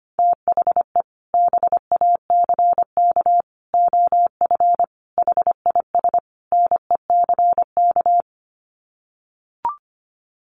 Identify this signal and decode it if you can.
Morse code, heard as a keyed tone, about 25 wpm, 700 Hz. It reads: THE BACK OF HIS NECK
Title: Sets of 5 Words Encore 25wpm